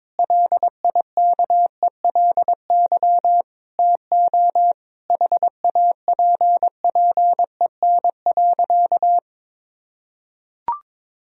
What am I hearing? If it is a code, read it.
Morse code, heard as a keyed tone, about 22 wpm, 700 Hz. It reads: LIKELY TO HAPPEN.
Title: Sentences from Top 1000 Words - Review Part 4 - 22wpm